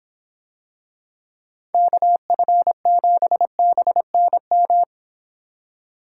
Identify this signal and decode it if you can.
Morse code, heard as a keyed tone, about 24 wpm, 700 Hz. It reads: KF7BNM